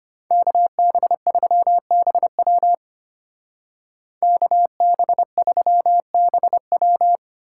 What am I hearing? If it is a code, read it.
Morse code, heard as a keyed tone, about 32 wpm, 700 Hz. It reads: KB3BW KB3BW